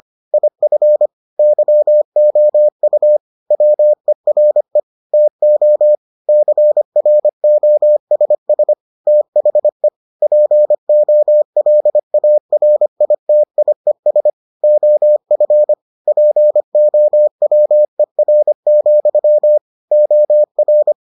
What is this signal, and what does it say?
Morse code, heard as a keyed tone, about 25 wpm, 600 Hz. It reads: IF YOU WERE TO CROSS THE POLARITIES OF POWER, OR